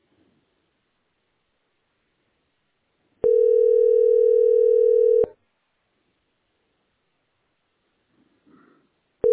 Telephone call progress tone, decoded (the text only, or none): ringback tone